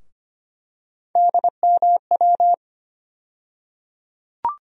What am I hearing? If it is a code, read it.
Morse code, heard as a keyed tone, about 25 wpm, 700 Hz. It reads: DMW
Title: USA Call Sign Postfixes - 3 Letters 25wpm